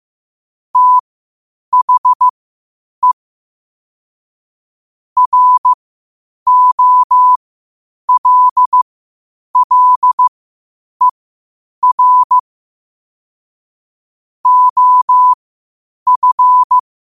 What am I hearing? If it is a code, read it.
Morse code, heard as a keyed tone, about 15 wpm, spaced as for 8 wpm, 1000 Hz. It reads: THE ROLLER OF
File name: qotdfiles / 8wpm